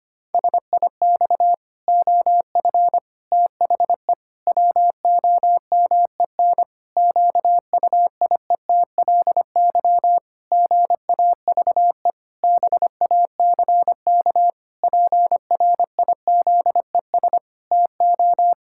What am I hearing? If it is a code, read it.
Morse code, heard as a keyed tone, about 25 wpm, 700 Hz. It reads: SIX OF THE WOMEN QUIETLY GAVE BACK PRIZES TO
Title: Pangram Challenge v2 - 25wpm